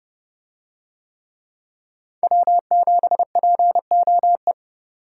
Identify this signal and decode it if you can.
Morse code, heard as a keyed tone, about 30 wpm, 700 Hz. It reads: W7POE